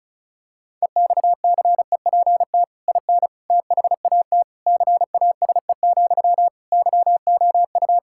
Morse code, heard as a keyed tone, about 36 wpm, 700 Hz. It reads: EXCEPT IN THAT CASE, YOU
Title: Sentences from Top 1600 Words 35wpm